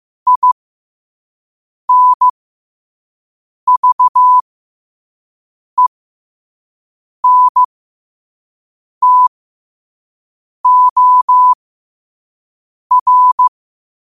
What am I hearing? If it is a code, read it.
Morse code, heard as a keyed tone, about 15 wpm, spaced as for 5 wpm, 1000 Hz. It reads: INVENTOR
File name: qotdfiles / 5wpm